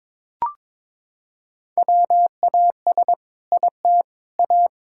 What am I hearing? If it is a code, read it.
Morse code, heard as a keyed tone, about 22 wpm, 700 Hz. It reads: WAS IT A
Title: Sentences from Top 1000 Words - Review Part 5 - 22wpm